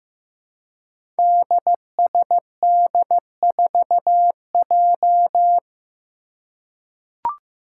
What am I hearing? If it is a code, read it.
Morse code, heard as a keyed tone, about 15 wpm, 700 Hz. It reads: DSD4J
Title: Five Challenging Letter-Number Combinations 15wpm